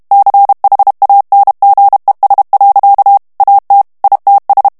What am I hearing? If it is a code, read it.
Morse code, heard as a keyed tone, about 32 wpm, 800 Hz. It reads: CHANGES. AT ITS
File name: ANGLAIS texte clair 3 vitesse 30 mots minute